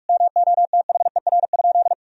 Morse code, heard as a keyed tone, about 46 wpm, 700 Hz. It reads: MOTHER?
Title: Sentences from Top 1000 Words - Review Part 2 - 45wpm